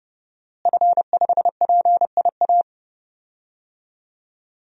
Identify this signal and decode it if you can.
Morse code, heard as a keyed tone, about 30 wpm, 700 Hz. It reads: F5PIA
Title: DXCC-100 Call Signs - 5 Characters - Repeated 3x - 30wpm